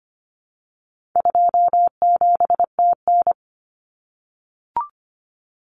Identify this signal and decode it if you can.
Morse code, heard as a keyed tone, about 25 wpm, 700 Hz. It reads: 27TN
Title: Common Cut Numbers - 4-Digit 25wpm